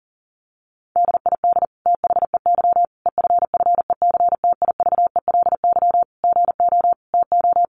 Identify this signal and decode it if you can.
Morse code, heard as a keyed tone, about 38 wpm, 700 Hz. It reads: DID THEY EFFECTIVELY GO TO
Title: Sentences from Top 1400 Words 40wpm